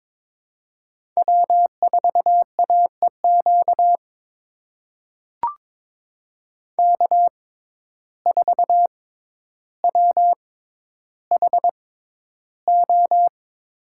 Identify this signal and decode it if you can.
Morse code, heard as a keyed tone, about 22 wpm, spaced as for 11 wpm, 700 Hz. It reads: W4AEQ K4WHO
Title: USA Call Signs - 5 Characters - 6x Character Spacing - 22wpm